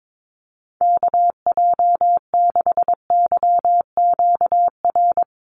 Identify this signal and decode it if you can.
Morse code, heard as a keyed tone, about 22 wpm, 700 Hz. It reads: KJ6YQR
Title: USA Call Signs 22wpm